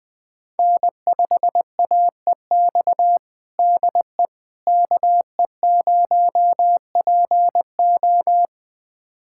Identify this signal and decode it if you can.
Morse code, heard as a keyed tone, about 20 wpm, 700 Hz. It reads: N5AEX DE KE0PO
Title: USA Callsign DE Callsign - 5 Characters - 20wpm